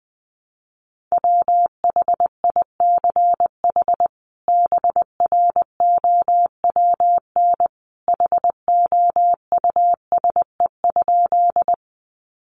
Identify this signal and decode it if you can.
Morse code, heard as a keyed tone, about 20 wpm, 700 Hz. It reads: WHICH BROWN HOUSE?
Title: Sentences from Top 1200 Words 20wpm